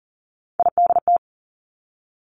Morse code, heard as a keyed tone, about 38 wpm, 700 Hz. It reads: IDT
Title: Three Challenging Letter-Number Combinations 40wpm